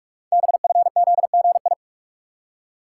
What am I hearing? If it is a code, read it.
Morse code, heard as a keyed tone, about 44 wpm, 700 Hz. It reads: BFZGI
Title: Code Groups - 5 Letters - 45wpm